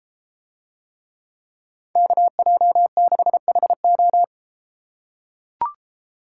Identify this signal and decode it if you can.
Morse code, heard as a keyed tone, about 32 wpm, 700 Hz. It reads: KJ6HO